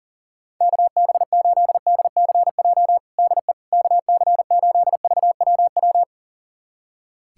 Morse code, heard as a keyed tone, about 37 wpm, 700 Hz. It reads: KB8DCJ DE KC8VWW
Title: USA Callsign DE Callsign - 6 Characters - 40wpm